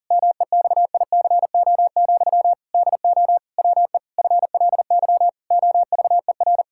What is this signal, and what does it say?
Morse code, heard as a keyed tone, about 37 wpm, 700 Hz. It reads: MEXICO, DO WE FLY OVER
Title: Pangram Challenge v2 - 40wpm